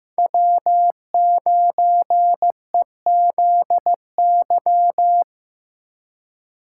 Morse code, heard as a keyed tone, about 15 wpm, 700 Hz. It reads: W9EZY
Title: USA Call Signs - 5 Characters - 3x Character Spacing - 15wpm